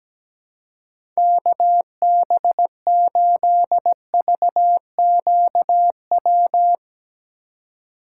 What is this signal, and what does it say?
Morse code, heard as a keyed tone, about 17 wpm, 700 Hz. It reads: KB8VQW
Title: USA Call Signs - 6 Characters - 2x Character Spacing - 17wpm